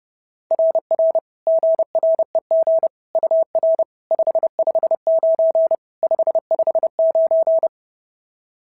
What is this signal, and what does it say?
Morse code, heard as a keyed tone, about 26 wpm, 650 Hz. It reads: RR GREG UR 559 559
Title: SOTA QSO Practice 30wpm